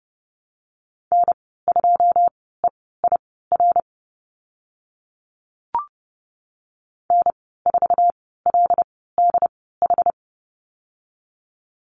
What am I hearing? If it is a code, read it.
Morse code, heard as a keyed tone, about 31 wpm, spaced as for 17 wpm, 700 Hz. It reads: N2EIR N4LDH
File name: Lesson 14g - Number 2 - Call Signs 30wpm